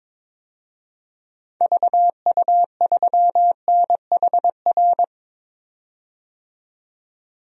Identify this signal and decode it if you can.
Morse code, heard as a keyed tone, about 22 wpm, 700 Hz. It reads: VU3NHR
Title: DXCC-100 Call Signs - 6 Characters - 22wpm